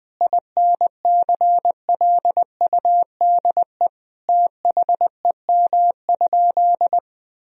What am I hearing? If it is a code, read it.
Morse code, heard as a keyed tone, about 20 wpm, 700 Hz. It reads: INCLUDE THEM?
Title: Sentences from Top 1000 Words - Review Part 3 - 20wpm